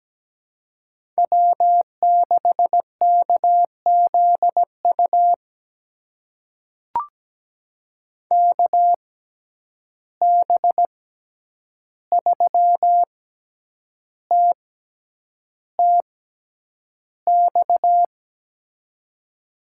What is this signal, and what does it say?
Morse code, heard as a keyed tone, about 17 wpm, spaced as for 8 wpm, 700 Hz. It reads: W6KZU KB3TTX
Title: USA Call Signs - 6x Character Spacing - 17wpm